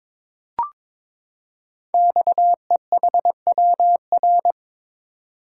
Morse code, heard as a keyed tone, about 22 wpm, 700 Hz. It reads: XEHWR
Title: Code Groups - 5 Letters - 22wpm